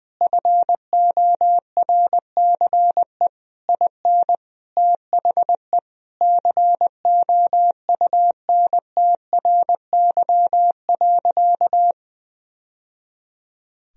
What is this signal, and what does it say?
Morse code, heard as a keyed tone, about 20 wpm, 700 Hz. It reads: FORCE IN THE COUNTRY.
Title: Sentences from Top 1000 Words - Review Part 4 - 20wpm